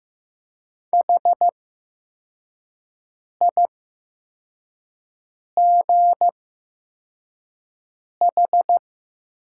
Morse code, heard as a keyed tone, about 15 wpm, spaced as for 4 wpm, 700 Hz. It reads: HIGH
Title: 8x Character Spacing - Top 200 Words - 15wpm